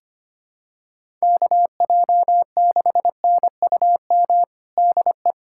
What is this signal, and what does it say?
Morse code, heard as a keyed tone, about 25 wpm, 700 Hz. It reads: KJ6NUM DE
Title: USA Callsign DE Callsign - 6 Characters - 25wpm